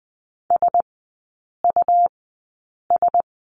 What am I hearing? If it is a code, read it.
Morse code, heard as a keyed tone, about 20 wpm, spaced as for 8 wpm, 700 Hz. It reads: SUS
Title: 100 Lesson 17e - Letter u - Three Characters 20wpm